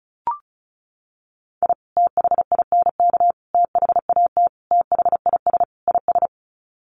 Morse code, heard as a keyed tone, about 34 wpm, 700 Hz. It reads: I THINK THAT THIS IS
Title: Sets of 5 Words Encore2 35wpm